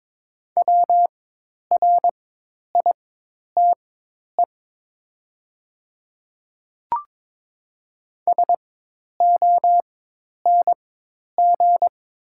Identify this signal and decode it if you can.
Morse code, heard as a keyed tone, about 22 wpm, spaced as for 10 wpm, 700 Hz. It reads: WRITE SONG
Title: 4x Character Spacing - Top 500 Words - 22wpm